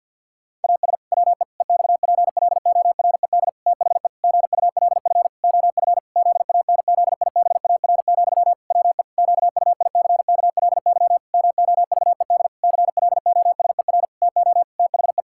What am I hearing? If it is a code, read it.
Morse code, heard as a keyed tone, about 50 wpm, 700 Hz. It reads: AS WE EXPLORED THE GULF OF ZANZIBAR, WE QUICKLY MOVED CLOSER TO THE
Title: Pangram Challenge v2 - 50wpm